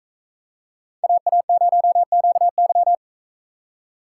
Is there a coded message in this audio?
Yes: Morse code from 1.0 to 3.0 s, heard as a keyed tone, about 41 wpm, 700 Hz. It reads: AA0QY